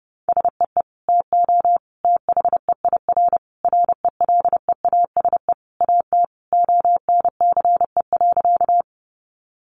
Morse code, heard as a keyed tone, about 30 wpm, 700 Hz. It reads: SEE TO THEIR RELEASE AT ONCE.
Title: Sentences from Top 1100 Words 30wpm